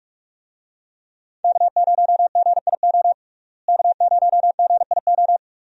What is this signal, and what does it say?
Morse code, heard as a keyed tone, about 41 wpm, 700 Hz. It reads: K0GIO K0GIO
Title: USA Call Signs - 5 Characters - Repeated 3x - 45wpm